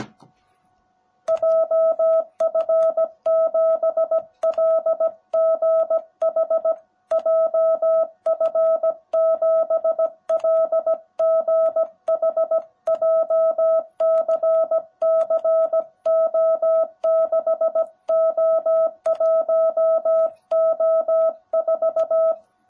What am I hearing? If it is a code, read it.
Morse code, heard as a keyed tone, about 17 wpm, 660 Hz. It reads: JF7LGH JF7LGH JCCO6O1O4
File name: jh8cbh / 2021cw / cw18b